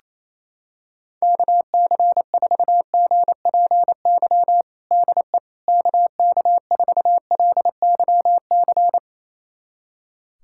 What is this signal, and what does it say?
Morse code, heard as a keyed tone, about 28 wpm, 700 Hz. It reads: KC4GPY DE KK4LYC
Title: USA Callsign DE Callsign - 6 Characters - 28wpm